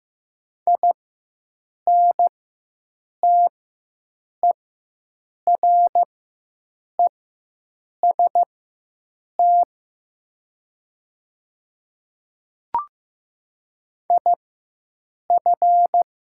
Morse code, heard as a keyed tone, about 15 wpm, spaced as for 7 wpm, 690 Hz. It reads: INTEREST IF
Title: 4x Character Spacing - Top 500 Words - 15wpm